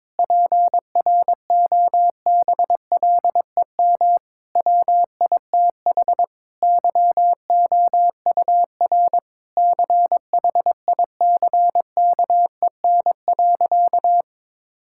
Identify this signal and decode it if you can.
Morse code, heard as a keyed tone, about 22 wpm, 700 Hz. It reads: PROBLEM WITH YOUR CHICKEN.
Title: Sentences from Top 1000 Words - Review Part 5 - 22wpm